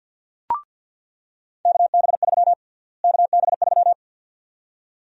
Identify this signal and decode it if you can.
Morse code, heard as a keyed tone, about 46 wpm, 700 Hz. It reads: KB3 KB3
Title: USA Call Sign Prefixes - Repeated 2x - 50wpm